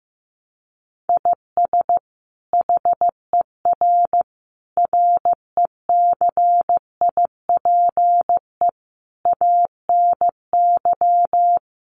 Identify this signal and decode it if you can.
Morse code, heard as a keyed tone, about 15 wpm, 700 Hz. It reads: IS HER RECIPE ANY
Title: Sentences from Top 1200 Words 15wpm